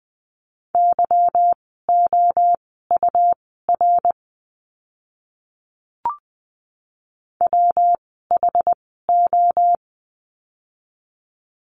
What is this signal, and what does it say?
Morse code, heard as a keyed tone, about 20 wpm, spaced as for 15 wpm, 700 Hz. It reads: YOUR WHO
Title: 237 Lesson 38f - Letter q - Words 20wpm